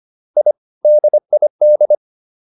Morse code, heard as a keyed tone, about 25 wpm, 600 Hz. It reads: I DID